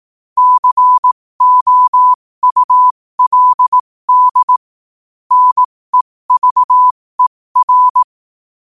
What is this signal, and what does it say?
Morse code, heard as a keyed tone, about 18 wpm, spaced as for 15 wpm, 1000 Hz. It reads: COULD NEVER
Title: 15 WPM morse code quotes for Fri, 01 Mar 2024